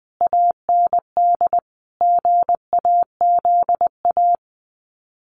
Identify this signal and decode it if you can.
Morse code, heard as a keyed tone, about 20 wpm, 700 Hz. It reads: AND GAZA
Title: Sets of 5 Words 20wpm